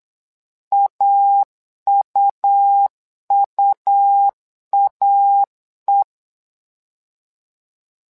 Morse code, heard as a keyed tone, about 8 wpm, 800 Hz. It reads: AUUAE
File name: LEZIONI / 2 UHDN / 2 UH 8wpm 2129